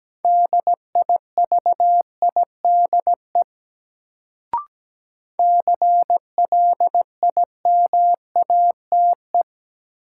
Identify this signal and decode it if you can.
Morse code, heard as a keyed tone, about 17 wpm, 700 Hz. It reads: DIVIDE CLIMATE
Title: Sentences from Top 1600 Words - New Word Drill Alt - 17wpm